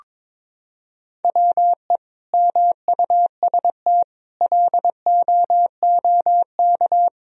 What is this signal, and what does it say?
Morse code, heard as a keyed tone, about 22 wpm, 700 Hz. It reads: WE MUST LOOK
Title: Sentences from Top 1000 Words - Review Part 2 - 22wpm